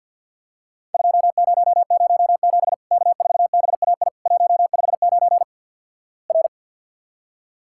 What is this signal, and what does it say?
Morse code, heard as a keyed tone, about 50 wpm, 700 Hz. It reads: 2007 K4BAI 159 R
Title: Worked All Europe DX Contest - QTC Exchange - 50wpm